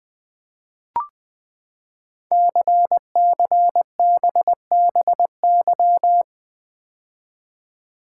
Morse code, heard as a keyed tone, about 20 wpm, 700 Hz. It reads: CCBBY
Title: Code Groups - 5 Letters - 20wpm